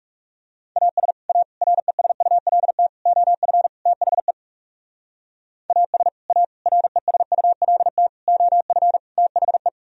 Morse code, heard as a keyed tone, about 37 wpm, 700 Hz. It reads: AS A RESULT OF THE AS A RESULT OF THE
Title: Speed-Racing - Sets of 5 Words 30wpm